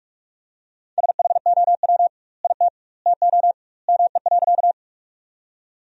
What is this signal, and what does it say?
Morse code, heard as a keyed tone, about 42 wpm, 700 Hz. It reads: SHOW IT TO ME.
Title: 2x Word-Spacing - Sentences from Top 200 Words - 45wpm